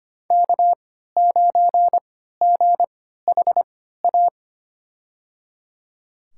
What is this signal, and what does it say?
Morse code, heard as a keyed tone, about 25 wpm, spaced as for 14 wpm, 700 Hz. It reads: K9GHA
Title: USA Call Signs - 3x Character Spacing - 25wpm